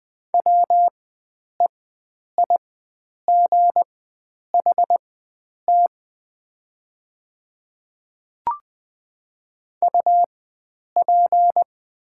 Morse code, heard as a keyed tone, about 21 wpm, spaced as for 9 wpm, 700 Hz. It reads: WEIGHT UP
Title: 4x Character Spacing - Top 500 Words - 20wpm